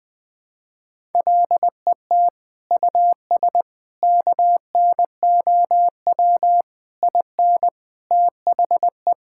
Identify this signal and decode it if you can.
Morse code, heard as a keyed tone, about 20 wpm, 700 Hz. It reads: LET US KNOW IN THE
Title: Sets of 6 Words - 20wpm